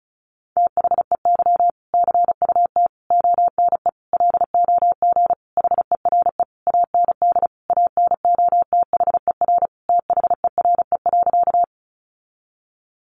Morse code, heard as a keyed tone, about 34 wpm, 700 Hz. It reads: THEY CUT ONE LOG HERE AND ANOTHER THERE.
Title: Sentences from Top 1400 Words 35wpm